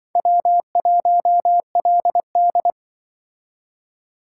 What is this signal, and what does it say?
Morse code, heard as a keyed tone, about 25 wpm, 700 Hz. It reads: W1LD
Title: Speed Racing - USA Call Signs 20wpm